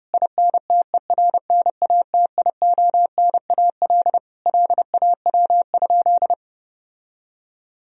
Morse code, heard as a keyed tone, about 30 wpm, 700 Hz. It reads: INTERNATIONAL LAW?
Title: Sentences from Top 1000 Words - Review Part 4 - 30wpm